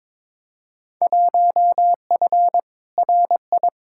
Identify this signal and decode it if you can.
Morse code, heard as a keyed tone, about 22 wpm, 700 Hz. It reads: 1F RI
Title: Field Day Exchanges Encore - 22wpm